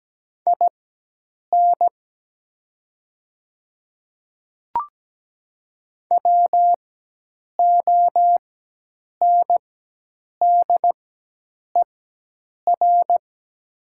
Morse code, heard as a keyed tone, about 17 wpm, spaced as for 8 wpm, 700 Hz. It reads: IN WONDER
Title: 4x Character Spacing - Top 500 Words - 17wpm